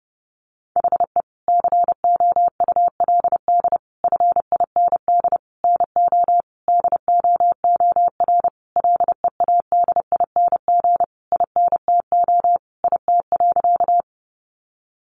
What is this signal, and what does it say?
Morse code, heard as a keyed tone, about 30 wpm, 700 Hz. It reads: HE COULD FIND NO DOOR LEADING INTO IT.
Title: Sentences from Top 1100 Words 30wpm